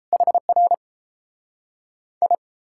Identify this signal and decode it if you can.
Morse code, heard as a keyed tone, about 38 wpm, 700 Hz. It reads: HR I